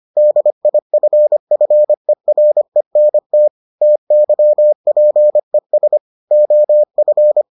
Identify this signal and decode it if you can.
Morse code, heard as a keyed tone, about 25 wpm, 600 Hz. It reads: DIFFERENT TYPES OF